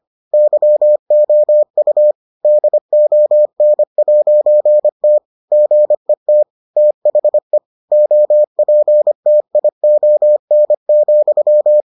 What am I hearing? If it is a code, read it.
Morse code, heard as a keyed tone, about 25 wpm, 600 Hz. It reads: YOU DON'T GET THE OPTION,